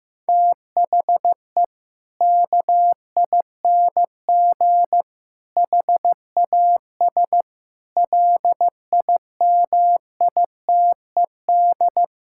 Morse code, heard as a keyed tone, about 16 wpm, 700 Hz. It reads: THE KING HAS LIMITED
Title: Sentences from Top 1000 Words - Review Part 6 - 15wpm